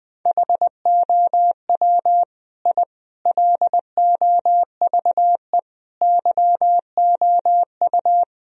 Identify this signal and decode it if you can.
Morse code, heard as a keyed tone, about 20 wpm, 700 Hz. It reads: HOW I LOVE YOU
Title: Sentences from Top 1000 Words - Review Part 1 - 20wpm